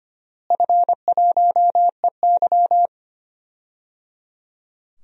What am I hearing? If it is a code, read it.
Morse code, heard as a keyed tone, about 25 wpm, 700 Hz. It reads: F1EY
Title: DXCC-100 Call Signs - 4 Characters - Repeated 3x - 25wpm